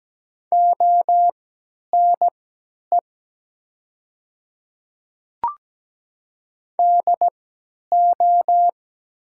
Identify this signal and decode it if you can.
Morse code, heard as a keyed tone, about 17 wpm, spaced as for 10 wpm, 700 Hz. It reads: ONE DO
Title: 3x Character Spacing - Top 200 Words - 17wpm